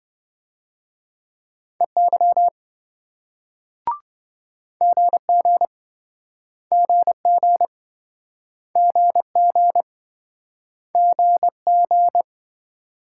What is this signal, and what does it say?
Morse code, heard as a keyed tone, about 27 wpm, 700 Hz. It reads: EY GG GG GG GG